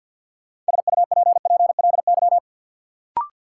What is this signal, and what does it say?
Morse code, heard as a keyed tone, about 53 wpm, 700 Hz. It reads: SUPPLY
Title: Sentences from Top 1200 Words - New Word Drill Alt - 50wpm